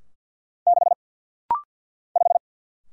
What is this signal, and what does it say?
Morse code, heard as a keyed tone, about 50 wpm, 690 Hz. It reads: 6 5
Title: Single Letter-Number Alt - Rapid-Fire 50wpm